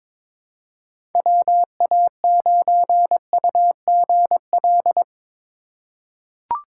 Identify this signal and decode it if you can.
Morse code, heard as a keyed tone, about 23 wpm, 700 Hz. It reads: WA9UGL
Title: USA Call Signs - 6 Characters - 3x Character Spacing - 22wpm